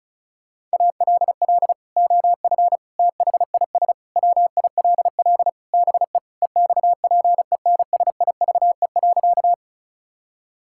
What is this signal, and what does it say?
Morse code, heard as a keyed tone, about 36 wpm, 700 Hz. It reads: ALL OF THIS WILL BE EXPENSIVE.
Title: Sentences from Top 1000 Words - Review Part 6 - 35wpm